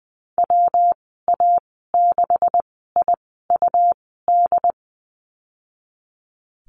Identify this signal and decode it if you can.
Morse code, heard as a keyed tone, about 20 wpm, spaced as for 15 wpm, 700 Hz. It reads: WA6IUD
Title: 167 Lesson 27g - Number 7 - Call Signs 20wpm